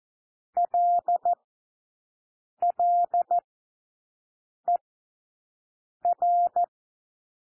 Morse code, heard as a keyed tone, about 14 wpm, 700 Hz. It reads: L L E R
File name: cwmp3 / W1 SOUND WORDS 14 wpm 0